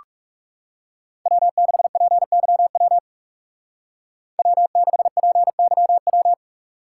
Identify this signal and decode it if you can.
Morse code, heard as a keyed tone, about 35 wpm, 700 Hz. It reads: W6PYW W6PYW
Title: Speed Racing - USA 1x3 Call Signs 30wpm